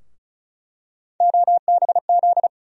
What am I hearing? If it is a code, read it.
Morse code, heard as a keyed tone, about 36 wpm, 700 Hz. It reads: OBZ
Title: USA Call Sign Postfixes - 3 Letters 35wpm